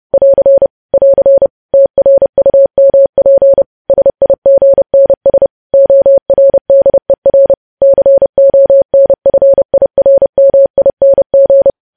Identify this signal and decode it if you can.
Morse code, heard as a keyed tone, about 30 wpm, 550 Hz. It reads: <AR> <AR> TRUMP SIGNS ORDER CONFIRMING
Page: news headlines in morse code at 30 wpm